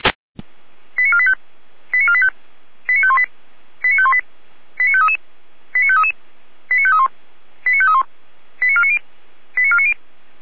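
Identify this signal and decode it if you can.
Five-tone selective call (ZVEI), heard as quick runs of five tones, tones of about 70 ms. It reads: 89485 89485 89518 89518 89533 89533 89531 89531 89490 89490
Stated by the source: ZVEI five-tone selective call received by SDR.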